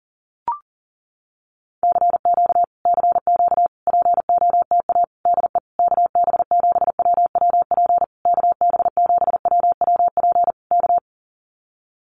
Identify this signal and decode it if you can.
Morse code, heard as a keyed tone, about 40 wpm, 700 Hz. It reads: CQ CQ POTA DE KB7WWP KB7WWP K
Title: Common QSO Phrases Encore 40wpm